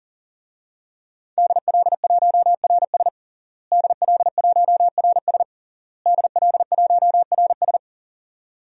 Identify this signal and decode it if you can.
Morse code, heard as a keyed tone, about 40 wpm, 700 Hz. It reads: DL1RS DL1RS DL1RS
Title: DXCC-100 Call Signs - 5 Characters - Repeated 3x - 40wpm